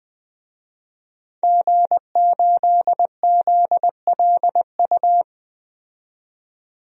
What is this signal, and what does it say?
Morse code, heard as a keyed tone, about 20 wpm, 700 Hz. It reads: G8ZLU
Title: DXCC-100 Call Signs - 5 Characters - 20wpm